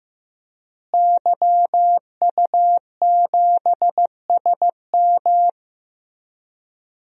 Morse code, heard as a keyed tone, about 15 wpm, 700 Hz. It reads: YU7SM
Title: DXCC-100 Call Signs - 5 Characters - Repeated 3x - 15wpm